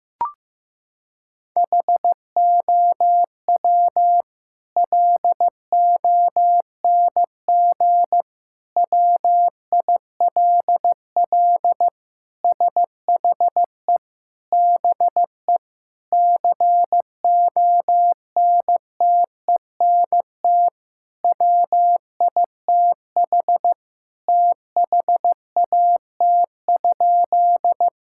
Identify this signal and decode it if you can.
Morse code, heard as a keyed tone, about 15 wpm, 700 Hz. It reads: HOW LONG WILL SHE BE CONTENT WITH THAT?
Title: Sentences from Top 1000 Words - Review Part 6 - 15wpm